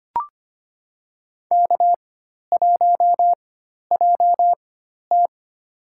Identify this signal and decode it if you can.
Morse code, heard as a keyed tone, about 25 wpm, spaced as for 12 wpm, 700 Hz. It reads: K1JT
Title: USA Call Signs - 4 Characters - 4x Character Spacing - 25wpm